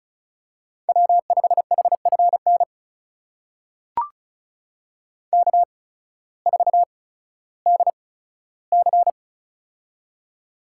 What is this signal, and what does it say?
Morse code, heard as a keyed tone, about 36 wpm, spaced as for 16 wpm, 700 Hz. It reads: W5HFN K4DC
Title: USA Call Signs - 8x Character Spacing - 35wpm